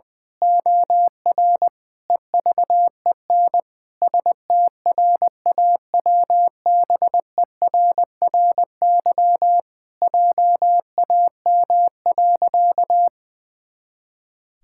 Morse code, heard as a keyed tone, about 20 wpm, 700 Hz. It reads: OR EVEN STRAWBERRY JAM.
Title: Pangram Challenge v2 - 20wpm